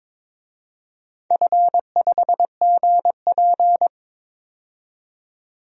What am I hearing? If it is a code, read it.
Morse code, heard as a keyed tone, about 22 wpm, 700 Hz. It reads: F5GP
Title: DXCC-100 Call Signs - 4 Characters - 22wpm